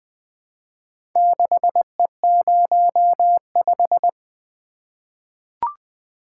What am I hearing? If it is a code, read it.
Morse code, heard as a keyed tone, about 21 wpm, 700 Hz. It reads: 6E05
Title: Four Letter-Number Combinations - 20wpm